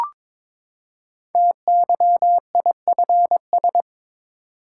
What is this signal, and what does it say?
Morse code, heard as a keyed tone, about 21 wpm, 700 Hz. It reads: TYIFS
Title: Code Groups - 5 Letter:Numbers - 22wpm